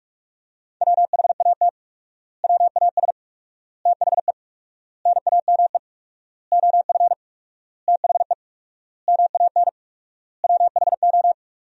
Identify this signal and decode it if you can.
Morse code, heard as a keyed tone, about 45 wpm, 700 Hz. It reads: WHAT WAS THE NAME OF THE MAN WHO
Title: 4x Word-Spacing - Sentences from Top 300 Words - 45wpm